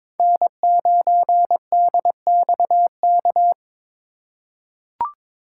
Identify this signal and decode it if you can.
Morse code, heard as a keyed tone, about 22 wpm, 700 Hz. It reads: N9DXK
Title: USA Call Signs - 5 Characters - 4x Character Spacing - 22wpm